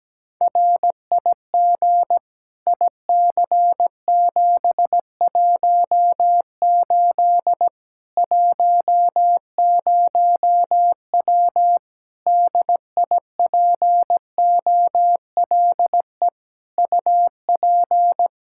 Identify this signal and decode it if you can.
Morse code, heard as a keyed tone, about 17 wpm, 700 Hz. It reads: RIG IC718 10W DIPOLE UP
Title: Practice QSO - Exchange 2 - Info - 17wpm